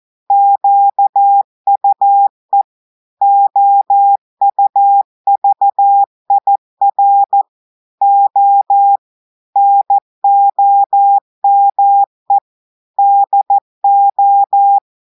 Morse code, heard as a keyed tone, about 14 wpm, 800 Hz. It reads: QUE OUVIR O NOME DO
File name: audios / portugais / portugais texte en clair 2 vitesse 14 mots minutes